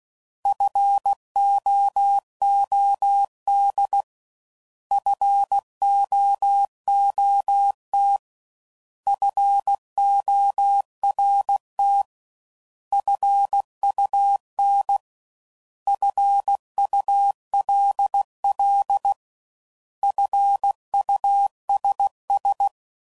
Morse code, heard as a keyed tone, about 16 wpm, 800 Hz. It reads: FOOD FOOT FORT FUN FULL FUSS
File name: Prefixes / By First Letter / Fs